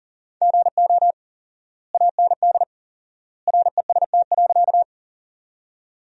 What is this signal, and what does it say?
Morse code, heard as a keyed tone, about 40 wpm, 700 Hz. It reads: GO AND REST.
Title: 4x Word-Spacing - Sentences from Top 500 Words - 40wpm